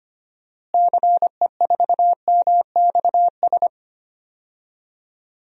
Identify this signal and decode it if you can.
Morse code, heard as a keyed tone, about 25 wpm, 700 Hz. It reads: CE4MXS
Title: DXCC-100 Call Signs - 6 Characters - Repeated 3x - 25wpm